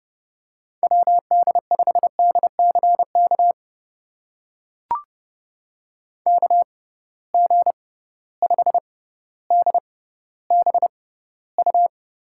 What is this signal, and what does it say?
Morse code, heard as a keyed tone, about 30 wpm, spaced as for 15 wpm, 700 Hz. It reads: WD5DCK KG5DBU
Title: USA Call Signs - 6 Characters - 6x Character Spacing - 30wpm